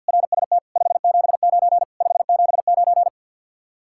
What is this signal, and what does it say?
Morse code, heard as a keyed tone, about 64 wpm, 700 Hz. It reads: RST 579 579